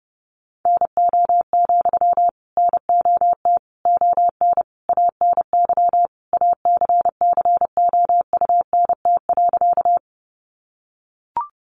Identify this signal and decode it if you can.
Morse code, heard as a keyed tone, about 30 wpm, 700 Hz. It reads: NO, NOT ON ANY ACCOUNT.
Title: Sentences from Top 900 Words 30wpm